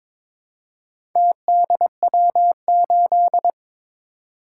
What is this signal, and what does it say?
Morse code, heard as a keyed tone, about 22 wpm, 700 Hz. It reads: TDW8
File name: Four Letter-Number Combinations - 22wpm